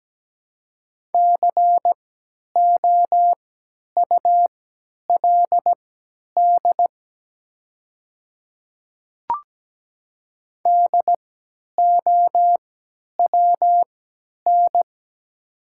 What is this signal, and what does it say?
Morse code, heard as a keyed tone, about 17 wpm, spaced as for 10 wpm, 690 Hz. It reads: COULD DOWN
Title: 3x Character Spacing - Top 100 Words - 17wpm